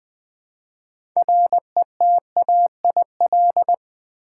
Morse code, heard as a keyed tone, about 20 wpm, 700 Hz. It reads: RETAIL
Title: Build A Word - 2 Letters At A Time - 6-Letter Words - 20wpm